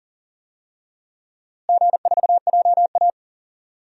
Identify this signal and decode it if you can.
Morse code, heard as a keyed tone, about 40 wpm, 700 Hz. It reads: G4JA
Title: DXCC-100 Call Signs - 4 Characters - Repeated 2x - 40wpm